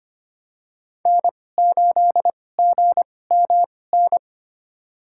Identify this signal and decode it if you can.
Morse code, heard as a keyed tone, about 25 wpm, spaced as for 18 wpm, 700 Hz. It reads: N8GMN
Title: USA Call Signs - 2x Character Spacing - 25wpm